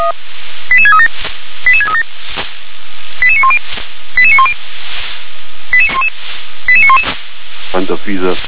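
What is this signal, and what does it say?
Five-tone selective call (ZVEI), heard as quick runs of five tones, tones of about 70 ms. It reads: 88637 88637 88910 88910 88911
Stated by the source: ZVEI five-tone selective call received by SDR.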